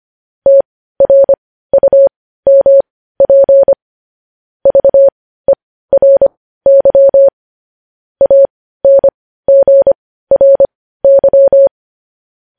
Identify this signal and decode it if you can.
Morse code, heard as a keyed tone, about 25 wpm, spaced as for 15 wpm, 550 Hz. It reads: TRUMP VERY ANGRY
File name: podcast / News headlines-1x-25x15